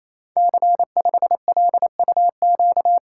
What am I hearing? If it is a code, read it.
Morse code, heard as a keyed tone, about 27 wpm, 700 Hz. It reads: C5LUQ
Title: Code Groups - 5 Letter:Numbers - 28wpm